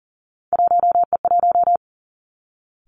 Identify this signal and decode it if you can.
Morse code, heard as a keyed tone, about 40 wpm, 700 Hz. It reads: 1E1
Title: Three Challenging Letter-Number Combinations 40wpm